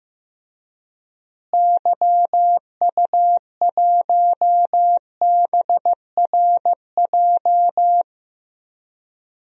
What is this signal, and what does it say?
Morse code, heard as a keyed tone, about 15 wpm, 700 Hz. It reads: YU1BRJ
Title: DXCC-100 Call Signs - 6 Characters - Repeated 2x - 15wpm